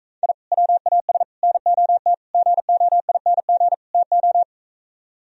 Morse code, heard as a keyed tone, about 42 wpm, 700 Hz. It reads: I WAS NOT GOING TO